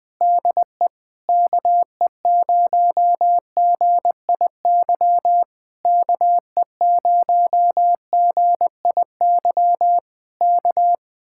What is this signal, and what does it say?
Morse code, heard as a keyed tone, about 20 wpm, 700 Hz. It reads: DE KE0GIY KE0GIY K
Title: Speed-Racing Common QSO Phrases - 20wpm